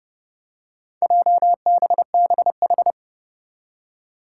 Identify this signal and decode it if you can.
Morse code, heard as a keyed tone, about 30 wpm, 700 Hz. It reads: JBBH
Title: Four Challenging Letter-Number Combinations 30wpm